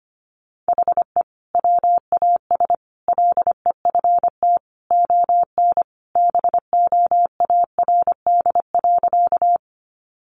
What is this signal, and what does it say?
Morse code, heard as a keyed tone, about 25 wpm, 700 Hz. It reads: HE WAS LEFT ON BOARD.
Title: Sentences from Top 700 Words 25wpm